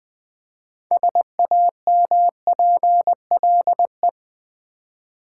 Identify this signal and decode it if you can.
Morse code, heard as a keyed tone, about 20 wpm, 700 Hz. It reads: SAMPLE
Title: Six Letter Words - 20wpm